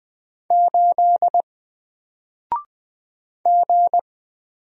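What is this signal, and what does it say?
Morse code, heard as a keyed tone, about 20 wpm, 700 Hz. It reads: 8 G
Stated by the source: Single letter and number Morse drill at 20 wpm, from N4LCD.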